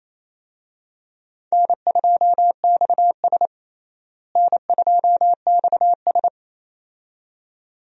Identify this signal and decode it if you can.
Morse code, heard as a keyed tone, about 28 wpm, 700 Hz. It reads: N2XS N2XS
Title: USA Call Signs - 4 Characters - Repeated 2x - 28wpm